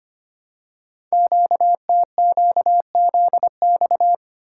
Morse code, heard as a keyed tone, about 25 wpm, 700 Hz. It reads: QTQZX
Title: Code Groups - 5 Letters - 25wpm